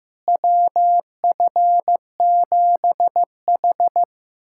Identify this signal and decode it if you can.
Morse code, heard as a keyed tone, about 15 wpm, 700 Hz. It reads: WF7H
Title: USA Callsign DE Callsign - 4 Characters - 15wpm